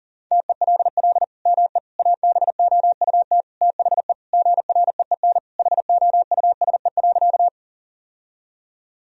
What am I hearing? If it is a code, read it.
Morse code, heard as a keyed tone, about 40 wpm, 700 Hz. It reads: TELL ME ABOUT THE GREEN HOUSE.
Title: Sentences from Top 1000 Words - Review Part 2 - 40wpm